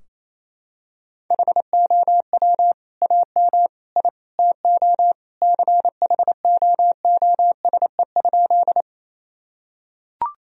Morse code, heard as a keyed tone, about 28 wpm, 700 Hz. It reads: HOW AM I TO CHOOSE?
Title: Sentences from Top 1000 Words - Review Part 4 - 28wpm